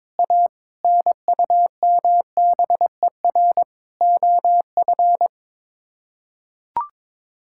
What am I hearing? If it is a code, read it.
Morse code, heard as a keyed tone, about 22 wpm, 700 Hz. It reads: A NUMBER OF
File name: practice / 22wpm / Sets of 6 Words - 22wpm